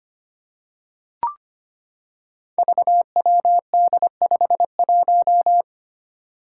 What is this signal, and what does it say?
Morse code, heard as a keyed tone, about 25 wpm, 700 Hz. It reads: VWD51
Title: Five Challenging Letter-Number Combinations 25wpm